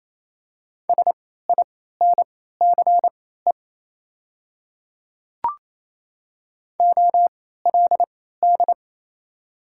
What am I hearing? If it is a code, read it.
Morse code, heard as a keyed tone, about 28 wpm, spaced as for 16 wpm, 700 Hz. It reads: SINCE OLD
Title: 3x Character Spacing - Top 300 Words - 28wpm